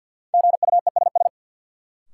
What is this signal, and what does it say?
Morse code, heard as a keyed tone, about 53 wpm, 700 Hz. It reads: GUESS
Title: Sentences from Top 1500 Words - New Word Drill - 50wpm